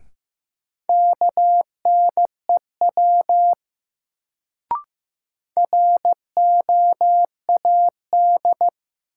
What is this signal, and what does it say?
Morse code, heard as a keyed tone, about 15 wpm, 700 Hz. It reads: KNEW ROAD
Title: Sentences from Top 400 Words - New Word Drill - 15wpm